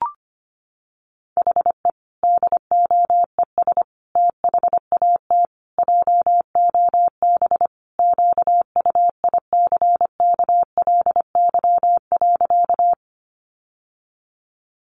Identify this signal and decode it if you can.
Morse code, heard as a keyed tone, about 25 wpm, 700 Hz. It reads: HE DOES THAT JOB QUICKLY.
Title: Sentences from Top 700 Words 25wpm